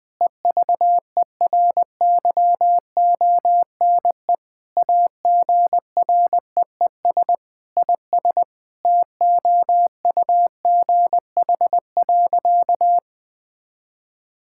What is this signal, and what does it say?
Morse code, heard as a keyed tone, about 20 wpm, 700 Hz. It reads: EVERYONE AGREES IS TOUGH.
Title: Sentences with 7-Word Phrases - 20wpm